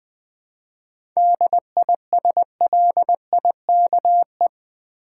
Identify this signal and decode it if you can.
Morse code, heard as a keyed tone, about 20 wpm, 700 Hz. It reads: DISLIKE
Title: 7 Letter Words - First 3 Letters Given - 20wpm